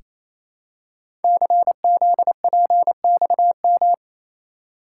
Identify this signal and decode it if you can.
Morse code, heard as a keyed tone, about 27 wpm, 700 Hz. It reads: CZPXM
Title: Code Groups - 5 Letters - 28wpm